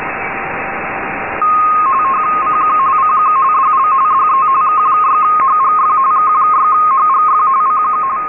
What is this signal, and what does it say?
RTTY, about 45.45 baud. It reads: CQ CQ DL DE K6HGF K6HGF CQ
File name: rtty-D2000Hz